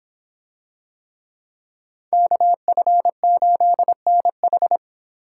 Morse code, heard as a keyed tone, about 24 wpm, 700 Hz. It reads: KF8NH